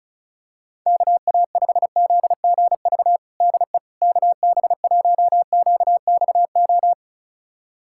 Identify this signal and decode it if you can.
Morse code, heard as a keyed tone, about 36 wpm, 700 Hz. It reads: KA5ZGV DE KB1QXO
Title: USA Callsign DE Callsign - 6 Characters - 35wpm